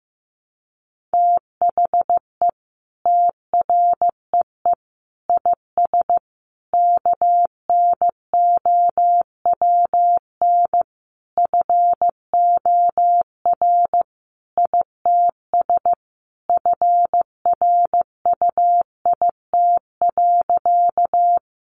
Morse code, heard as a keyed tone, about 15 wpm, 700 Hz. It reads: THE TREE IS KNOWN FOR ITS FRUIT.
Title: Sentences from Top 1300 Words 15wpm